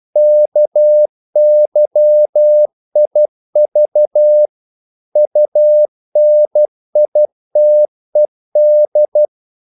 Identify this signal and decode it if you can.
Morse code, heard as a keyed tone, about 12 wpm, 600 Hz. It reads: KYIV UNITED
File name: CW / Practice / NationalCapitals12